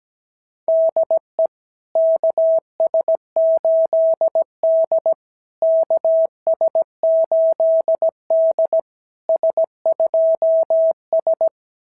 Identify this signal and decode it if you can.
Morse code, heard as a keyed tone, about 17 wpm, 660 Hz. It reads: DE KS8D KS8D S2S
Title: SOTA QSO Practice 17wpm